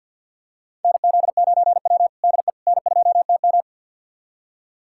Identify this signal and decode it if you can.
Morse code, heard as a keyed tone, about 51 wpm, 700 Hz. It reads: NZ9W DE N2TM
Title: USA Callsign DE Callsign - 4 Characters - 50wpm